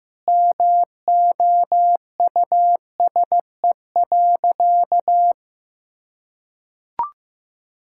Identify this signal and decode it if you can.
Morse code, heard as a keyed tone, about 15 wpm, 700 Hz. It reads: MOUSE.
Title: Sentences from Top 1000 Words - Review Part 5 - 15wpm